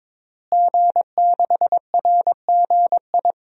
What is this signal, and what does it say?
Morse code, heard as a keyed tone, about 22 wpm, 700 Hz. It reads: G6RGI
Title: DXCC-100 Call Signs - 5 Characters - Repeated 2x - 22wpm